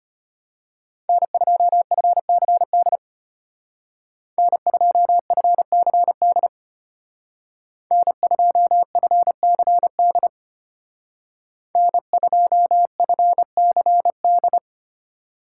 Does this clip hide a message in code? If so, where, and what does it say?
Yes: Morse code from 1.1 to 14.6 s, heard as a keyed tone, about 33 wpm, 700 Hz. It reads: N2FCD N2FCD N2FCD N2FCD